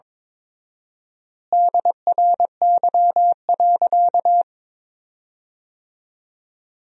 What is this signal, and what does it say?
Morse code, heard as a keyed tone, about 22 wpm, 700 Hz. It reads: DRY.
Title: 4x Word-Spacing - Sentences from Top 500 Words - 22wpm